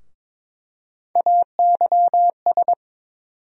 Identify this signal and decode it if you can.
Morse code, heard as a keyed tone, about 22 wpm, 700 Hz. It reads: AYS
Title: USA Call Sign Postfixes - 3 Letters 22wpm